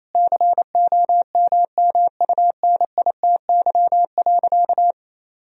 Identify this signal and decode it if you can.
Morse code, heard as a keyed tone, about 28 wpm, 700 Hz. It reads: COMMUNITY.
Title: Sentences from Top 1000 Words - Review Part 4 - 28wpm